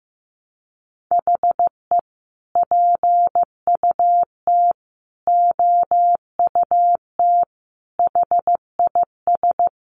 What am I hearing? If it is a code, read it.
Morse code, heard as a keyed tone, about 15 wpm, 700 Hz. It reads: HE PUT OUT HIS
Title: Sentences from Top 800 Words 15wpm